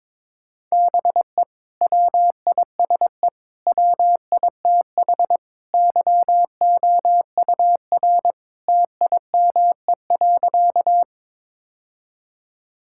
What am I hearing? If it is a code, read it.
Morse code, heard as a keyed tone, about 21 wpm, 700 Hz. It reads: BE WISE WITH YOUR TIME.
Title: Sentences from Top 1600 Words 22wpm